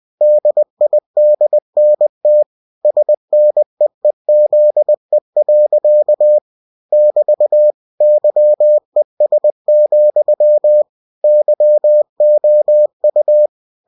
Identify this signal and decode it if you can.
Morse code, heard as a keyed tone, about 20 wpm, 600 Hz. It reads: DIDNT SNEEZE. <BT> YES, YOU
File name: cw-books / winnie-the-pooh / winnie-20wpm-page-0030